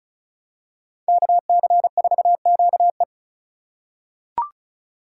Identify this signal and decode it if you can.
Morse code, heard as a keyed tone, about 36 wpm, 700 Hz. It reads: KC4QE
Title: USA Call Signs - 3x Character Spacing - 35wpm